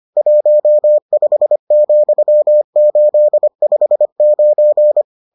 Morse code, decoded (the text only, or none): 15,859